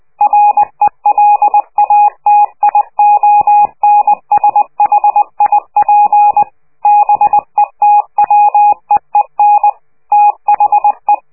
Morse code, heard as a keyed tone, about 20 wpm, 830 Hz. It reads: RELATIONSHIP BETWEEN THE
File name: worldcoastal / scheveningenradio / morsepch